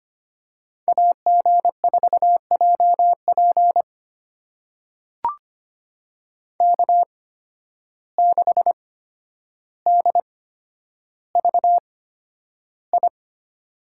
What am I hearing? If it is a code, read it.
Morse code, heard as a keyed tone, about 25 wpm, spaced as for 11 wpm, 700 Hz. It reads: AG4JP K6DVI
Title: USA Call Signs - 5 Characters - 8x Character Spacing - 25wpm